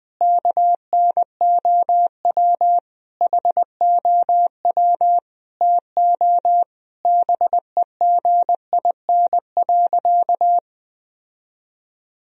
Morse code, heard as a keyed tone, about 20 wpm, 700 Hz. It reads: KNOW HOW TO BEGIN.
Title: Sentences from Top 1000 Words - Review Part 1 - 20wpm